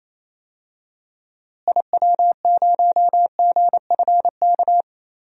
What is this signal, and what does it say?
Morse code, heard as a keyed tone, about 28 wpm, 700 Hz. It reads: IW0GFK
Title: DXCC-100 Call Signs - 6 Characters - Repeated 2x - 28wpm